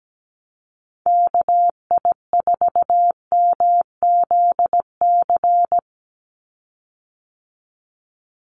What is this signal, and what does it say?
Morse code, heard as a keyed tone, about 17 wpm, 700 Hz. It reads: KI4MZC
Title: Call Signs Repeated 2x - 17wpm